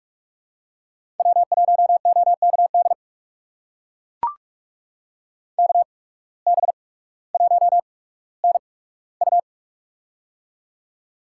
Morse code, heard as a keyed tone, about 44 wpm, spaced as for 19 wpm, 700 Hz. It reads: W1OKD KB1NU
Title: USA Call Signs - 5 Characters - 8x Character Spacing - 45wpm